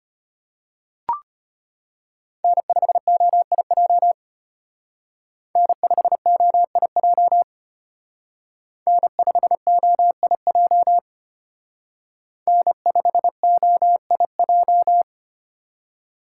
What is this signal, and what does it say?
Morse code, heard as a keyed tone, about 29 wpm, 700 Hz. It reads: N5OIJ N5OIJ N5OIJ N5OIJ
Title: Speed Racing - USA 1x3 Call Signs 25wpm